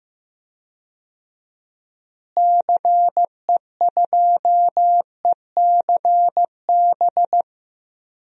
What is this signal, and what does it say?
Morse code, heard as a keyed tone, about 15 wpm, 700 Hz. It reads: CE2ECB
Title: DXCC-100 Call Signs - 6 Characters - Repeated 3x - 15wpm